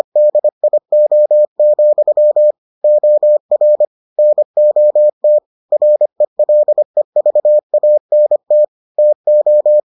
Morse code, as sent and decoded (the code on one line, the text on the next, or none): -.. .. --- --..-- / --- .-. / -. --- - / .-. .-.. ...- .- -. - / - ---
DIO, OR NOT RELEVANT TO